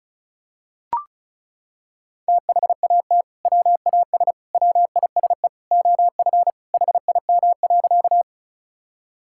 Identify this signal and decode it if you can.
Morse code, heard as a keyed tone, about 36 wpm, 700 Hz. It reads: THAT WAS WISE OF HIM.
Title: Sentences from Top 1600 Words 35wpm